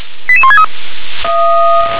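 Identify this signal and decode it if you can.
Five-tone selective call (ZVEI), heard as quick runs of five tones, tones of about 70 ms. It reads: 88163
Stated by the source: ZVEI five-tone selective call received by SDR.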